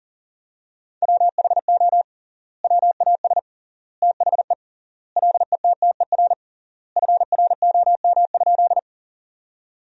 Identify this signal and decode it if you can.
Morse code, heard as a keyed tone, about 40 wpm, 700 Hz. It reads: WHO WAS THE LETTER FROM?
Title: 3x Word-Spacing - Sentences from Top 300 Words - 40wpm